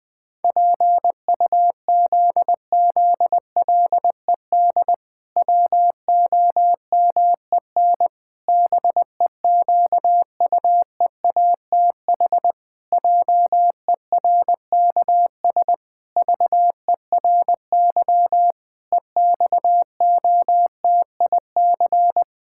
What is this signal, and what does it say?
Morse code, heard as a keyed tone, about 20 wpm, 700 Hz. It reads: PUZZLED WOMEN BEQUEATH JERKS VERY EXOTIC
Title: Pangram Challenge v2 - 20wpm